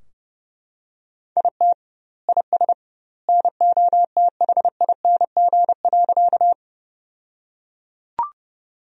Morse code, heard as a keyed tone, about 30 wpm, 700 Hz. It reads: IT IS NOTHING.
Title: 2x Word-Spacing - Sentences from Top 500 Words - 30wpm